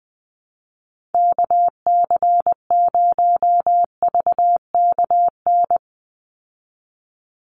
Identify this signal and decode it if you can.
Morse code, heard as a keyed tone, about 20 wpm, 700 Hz. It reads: KC0VKN
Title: Call Signs Repeated 2x - 20wpm